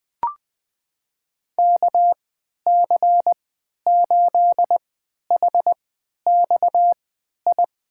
Morse code, heard as a keyed tone, about 20 wpm, spaced as for 11 wpm, 700 Hz. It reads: KC8HXI
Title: USA Call Signs - 6 Characters - 3x Character Spacing - 20wpm